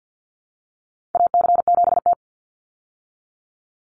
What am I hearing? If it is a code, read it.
Morse code, heard as a keyed tone, about 49 wpm, 700 Hz. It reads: AC7T
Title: Call Signs Repeated 2x - 50wpm